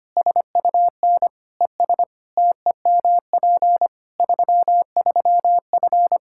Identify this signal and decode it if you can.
Morse code, heard as a keyed tone, about 25 wpm, 700 Hz. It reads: SUN ES TEMP 33F
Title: K1USN SST QSO Practice 25wpm